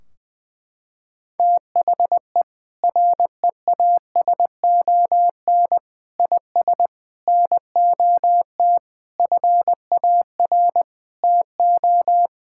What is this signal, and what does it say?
Morse code, heard as a keyed tone, about 20 wpm, 700 Hz. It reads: THE REASON IS NOT FAR TO
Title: Sentences from Top 1600 Words 20wpm